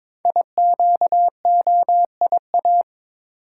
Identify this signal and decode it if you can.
Morse code, heard as a keyed tone, about 22 wpm, 700 Hz. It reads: IQOIA
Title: Code Groups - 5 Letters - 22wpm